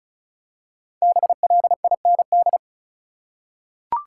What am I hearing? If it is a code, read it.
Morse code, heard as a keyed tone, about 34 wpm, 700 Hz. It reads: BLIND
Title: Five Letter Words - 35wpm